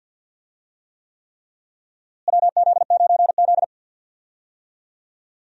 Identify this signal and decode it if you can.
Morse code, heard as a keyed tone, about 53 wpm, 700 Hz. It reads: WZ9Z
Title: USA Call Signs - 4 Characters - Repeated 3x - 50wpm